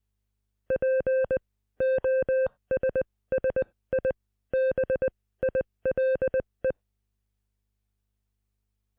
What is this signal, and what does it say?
Morse code, heard as a keyed tone, about 20 wpm, spaced as for 15 wpm, 520 Hz. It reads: POSSIBILE